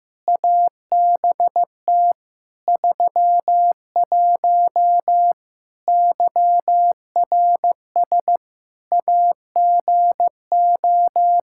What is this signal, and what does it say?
Morse code, heard as a keyed tone, about 15 wpm, 700 Hz. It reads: ABT 31 YRS AGO
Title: K1USN SST QSO Practice 15wpm